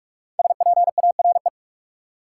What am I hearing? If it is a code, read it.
Morse code, heard as a keyed tone, about 42 wpm, 700 Hz. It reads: SPARE
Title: Five Letter Words - 45wpm